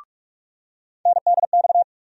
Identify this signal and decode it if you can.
Morse code, heard as a keyed tone, about 46 wpm, 700 Hz. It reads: NDX
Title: USA Call Sign Postfixes - 3 Letters 45wpm